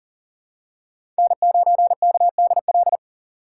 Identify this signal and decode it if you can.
Morse code, heard as a keyed tone, about 40 wpm, 700 Hz. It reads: N9KDL
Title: USA Call Signs - 5 Characters - 3x Character Spacing - 40wpm